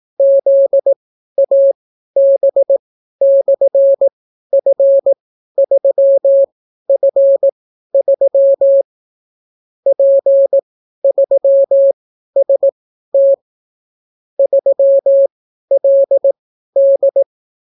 Morse code, heard as a keyed tone, about 18 wpm, spaced as for 12 wpm, 550 Hz. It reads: ZAB/F3F3 P3ST 3LD